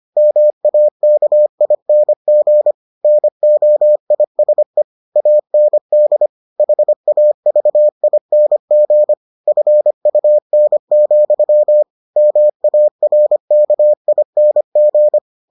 Morse code, heard as a keyed tone, about 25 wpm, 600 Hz. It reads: MAKING NOISE AND HAVING FUN, MARKING